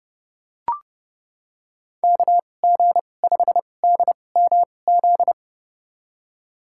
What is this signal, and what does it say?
Morse code, heard as a keyed tone, about 30 wpm, spaced as for 22 wpm, 700 Hz. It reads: KG5DMZ
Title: USA Call Signs - 6 Characters - 2x Character Spacing - 30wpm